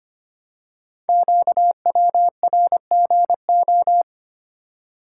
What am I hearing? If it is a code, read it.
Morse code, heard as a keyed tone, about 25 wpm, 700 Hz. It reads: QWRGO
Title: Code Groups - 5 Letters - 25wpm